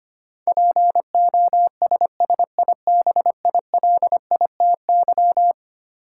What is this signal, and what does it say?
Morse code, heard as a keyed tone, about 25 wpm, 700 Hz. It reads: POSSIBILITY
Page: morse-code-ninja-practice-25wpm